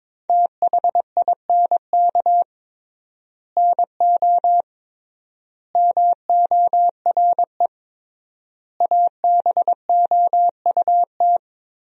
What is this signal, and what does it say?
Morse code, heard as a keyed tone, about 21 wpm, 700 Hz. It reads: THINK NO MORE ABOUT
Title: 3x Word-Spacing - Sentences from Top 200 Words - 22wpm